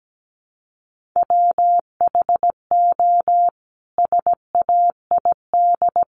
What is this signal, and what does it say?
Morse code, heard as a keyed tone, about 17 wpm, 700 Hz. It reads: WHO SAID
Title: Sentences from Top 700 Words 17wpm